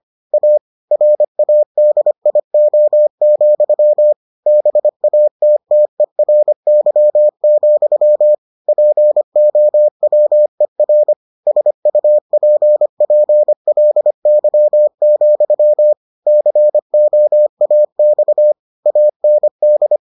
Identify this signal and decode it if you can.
Morse code, heard as a keyed tone, about 25 wpm, 600 Hz. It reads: A RADIO, BATTERY, POWER SUPPLY, COAX AND